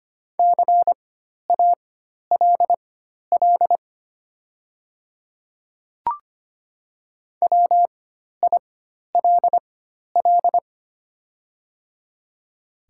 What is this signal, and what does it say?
Morse code, heard as a keyed tone, about 25 wpm, spaced as for 12 wpm, 700 Hz. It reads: CALL WILL
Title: 4x Character Spacing - Top 100 Words - 25wpm